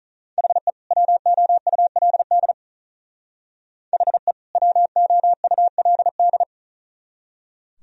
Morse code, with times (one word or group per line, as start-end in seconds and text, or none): 0.4-0.7 HE
0.9-2.5 WOULD
3.9-4.3 HE
4.6-6.4 WOULD